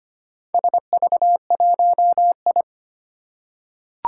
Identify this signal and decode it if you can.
Morse code, heard as a keyed tone, about 25 wpm, 700 Hz. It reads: SV1I
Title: Four Challenging Letter-Number Combinations 25wpm